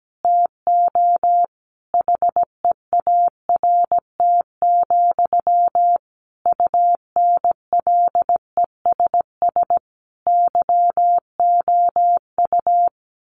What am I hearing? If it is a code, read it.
Morse code, heard as a keyed tone, about 17 wpm, 700 Hz. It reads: TO HEART, UNLESS YOU
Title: Sentences from Top 1500 Words 17wpm